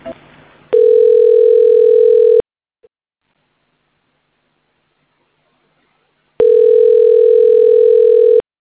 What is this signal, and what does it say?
Ringback tone (North America), 440 + 480 Hz.